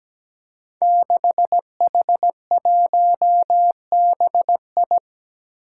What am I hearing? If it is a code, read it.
Morse code, heard as a keyed tone, about 17 wpm, 700 Hz. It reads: 6H1BI
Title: Five Challenging Letter-Number Combinations 17wpm